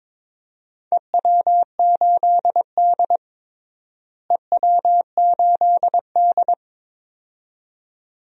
Morse code, heard as a keyed tone, about 21 wpm, 700 Hz. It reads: EW8D EW8D
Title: DXCC-100 Call Signs - 4 Characters - Repeated 2x - 22wpm